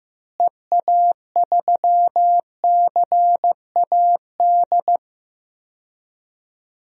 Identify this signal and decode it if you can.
Morse code, heard as a keyed tone, about 15 wpm, 700 Hz. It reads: EA3CAD
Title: DXCC-100 Call Signs - 6 Characters - Repeated 3x - 15wpm